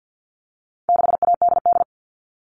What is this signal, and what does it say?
Morse code, heard as a keyed tone, about 50 wpm, 700 Hz. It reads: 6ADD
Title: All Cut Numbers - 4-Digit 50wpm